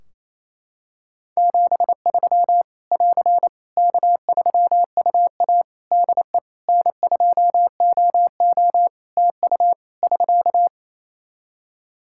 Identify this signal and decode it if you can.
Morse code, heard as a keyed tone, about 28 wpm, 700 Hz. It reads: 73 <AR> K3UA DE N2OO TU <SK>
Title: K1USN SST QSO Practice 28wpm